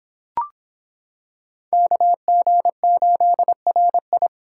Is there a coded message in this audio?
Yes: Morse code from 1.7 to 4.3 s, heard as a keyed tone, about 26 wpm, 700 Hz. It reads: KG8RI